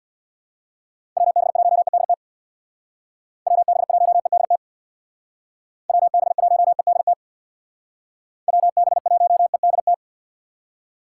Morse code, heard as a keyed tone, about 64 wpm, 700 Hz. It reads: WB1EDT WB1EDT WB1EDT WB1EDT